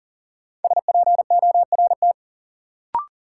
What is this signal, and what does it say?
Morse code, heard as a keyed tone, about 39 wpm, 700 Hz. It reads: SPORT
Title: Sentences from Top 1500 Words - New Word Drill Alt - 40wpm